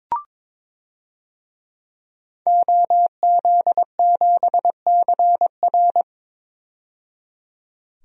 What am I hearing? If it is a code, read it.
Morse code, heard as a keyed tone, about 22 wpm, 700 Hz. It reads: OZ7CR
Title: DXCC-100 Call Signs - 5 Characters - 22wpm